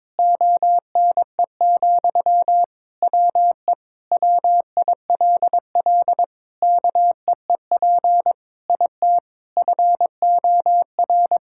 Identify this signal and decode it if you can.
Morse code, heard as a keyed tone, about 22 wpm, 700 Hz. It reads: ONE, WE WILL KEEP IT FOR
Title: Sentences from Top 1600 Words 22wpm